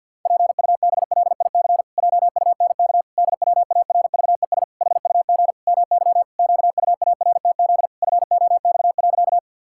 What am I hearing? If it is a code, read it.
Morse code, heard as a keyed tone, about 52 wpm, 700 Hz. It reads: PUBLIC JUNK DWARVES HUG MY QUARTZ FOX.
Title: Pangram Challenge v2 - 50wpm